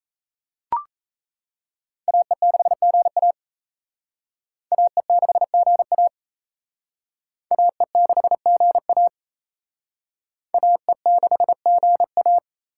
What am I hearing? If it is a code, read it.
Morse code, heard as a keyed tone, about 37 wpm, 700 Hz. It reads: AE6GA AE6GA AE6GA AE6GA